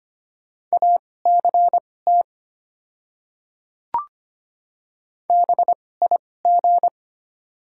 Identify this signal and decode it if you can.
Morse code, heard as a keyed tone, about 25 wpm, spaced as for 18 wpm, 690 Hz. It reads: ACT BIG
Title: 2x Character Spacing - Top 200 Words - 25wpm